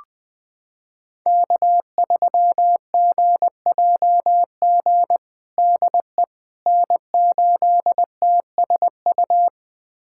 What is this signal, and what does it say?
Morse code, heard as a keyed tone, about 19 wpm, 700 Hz. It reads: K3GJG DE N8TSU
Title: USA Callsign DE Callsign - 5 Characters - 20wpm